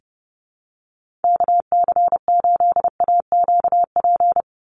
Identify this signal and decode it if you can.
Morse code, heard as a keyed tone, about 30 wpm, 700 Hz. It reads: KC8AQP
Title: USA Call Signs 30wpm